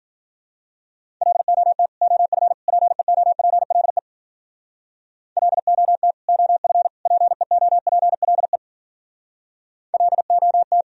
Morse code, heard as a keyed tone, about 46 wpm, 700 Hz. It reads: LOT OF PEOPLE LOT OF PEOPLE LOT
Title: Speed-Racing - Sets of 3 Words 40wpm